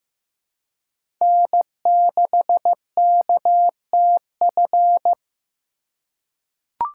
Morse code, heard as a keyed tone, about 15 wpm, 700 Hz. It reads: N6KTF
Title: USA Call Signs - 5 Characters - 2x Character Spacing - 15wpm